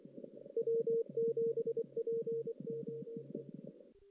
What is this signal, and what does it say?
Morse code, heard as a keyed tone, about 25 wpm, 460 Hz. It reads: W7POE
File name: 2016 JAN CW NAQP